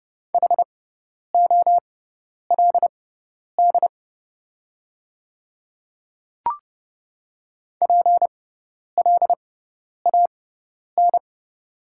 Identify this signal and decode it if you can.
Morse code, heard as a keyed tone, about 30 wpm, spaced as for 10 wpm, 690 Hz. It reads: HOLD PLAN
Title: 6x Character Spacing - Top 500 Words - 30wpm